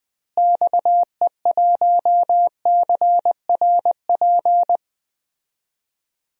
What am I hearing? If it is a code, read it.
Morse code, heard as a keyed tone, about 20 wpm, 700 Hz. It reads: XE1CRP
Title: DXCC-100 Call Signs - 6 Characters - Repeated 3x - 20wpm